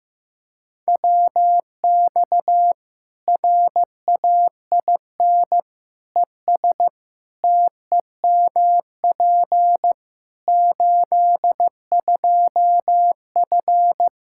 Morse code, heard as a keyed tone, about 15 wpm, 700 Hz. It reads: WX RAIN ES TEMP 82F
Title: Practice QSO - Exchange 2 - Info - 15wpm